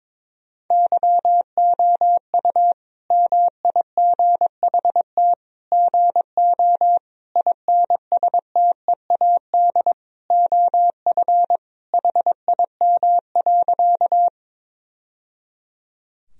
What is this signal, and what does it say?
Morse code, heard as a keyed tone, about 22 wpm, 700 Hz. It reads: YOU MIGHT GO INSTEAD OF HIM.
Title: Sentences from Top 1000 Words - Review Part 4 - 22wpm